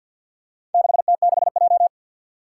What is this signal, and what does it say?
Morse code, heard as a keyed tone, about 51 wpm, 700 Hz. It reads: 6T6J
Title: Four Challenging Letter-Number Combinations 50wpm